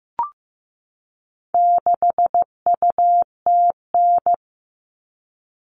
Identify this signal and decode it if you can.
Morse code, heard as a keyed tone, about 15 wpm, 700 Hz. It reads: 6UTN
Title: All Cut Numbers - 4-Digit 15wpm